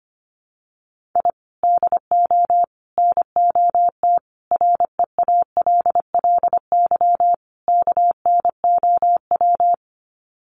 Morse code, heard as a keyed tone, about 25 wpm, 700 Hz. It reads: I DO NOT REALLY KNOW
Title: Sets of 5 Words Encore 25wpm